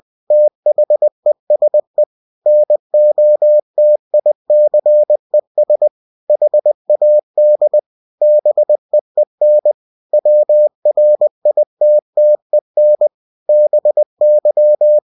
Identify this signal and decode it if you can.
Morse code, heard as a keyed tone, about 20 wpm, 600 Hz. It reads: THESE NOTICES HAD BEEN WRITTEN BY
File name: cw-books / winnie-the-pooh / winnie-20wpm-page-0029